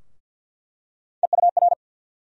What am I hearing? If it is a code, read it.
Morse code, heard as a keyed tone, about 48 wpm, 700 Hz. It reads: EUR
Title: USA Call Sign Postfixes - 3 Letters 50wpm